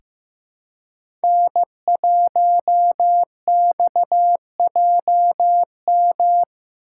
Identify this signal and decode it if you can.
Morse code, heard as a keyed tone, about 15 wpm, 700 Hz. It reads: N1XJM
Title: K1USN SST QSO Practice - 15wpm